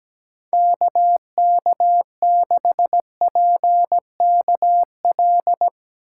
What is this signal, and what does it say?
Morse code, heard as a keyed tone, about 17 wpm, 700 Hz. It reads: KK6PKL
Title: USA Call Signs - 6 Characters - 3x Character Spacing - 17wpm